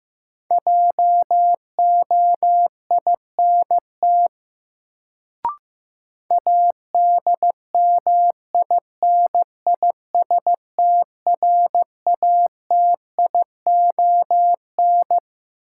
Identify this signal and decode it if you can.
Morse code, heard as a keyed tone, about 15 wpm, 700 Hz. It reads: JOINT ADMINISTRATION
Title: Sentences from Top 1300 Words - New Word Drill - 15wpm